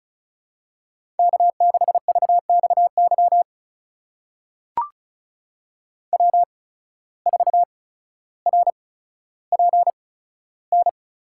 Morse code, heard as a keyed tone, about 36 wpm, spaced as for 15 wpm, 700 Hz. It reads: K6VXY W4RPN
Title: USA Call Signs - 5 Characters - 8x Character Spacing - 35wpm